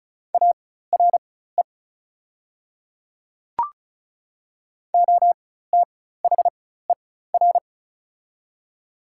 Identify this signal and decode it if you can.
Morse code, heard as a keyed tone, about 36 wpm, spaced as for 16 wpm, 700 Hz. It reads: ARE OTHER
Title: 4x Character Spacing - Top 300 Words - 35wpm